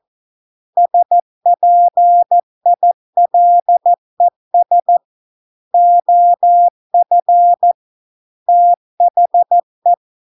Morse code, heard as a keyed tone, about 14 wpm, 700 Hz. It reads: SPILES OF THE